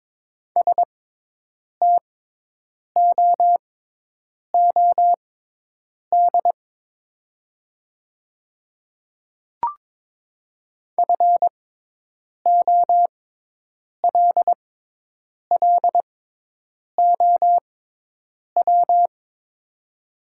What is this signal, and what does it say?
Morse code, heard as a keyed tone, about 23 wpm, spaced as for 8 wpm, 700 Hz. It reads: STOOD FOLLOW
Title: 6x Character Spacing - Top 500 Words - 22wpm